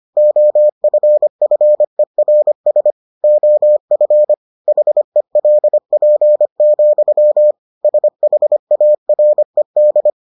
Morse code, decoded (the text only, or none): OFFERS OF HELP, SHARED